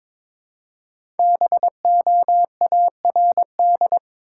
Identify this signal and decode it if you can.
Morse code, heard as a keyed tone, about 22 wpm, 700 Hz. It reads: BOARD
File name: 5 Letter Words - First 3 Letters Given - 22wpm